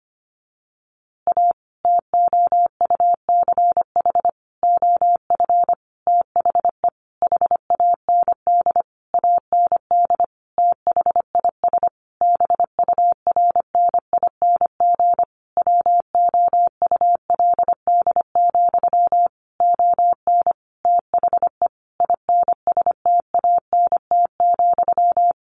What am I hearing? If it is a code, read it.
Morse code, heard as a keyed tone, about 25 wpm, 700 Hz. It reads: A TOUCH OF THE HAND AND THIS BURNING WOULD, ON THE INSTANT,
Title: A Sound Of Thunder 25wpm